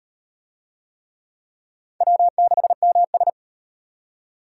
Morse code, heard as a keyed tone, about 34 wpm, 700 Hz. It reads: W6MS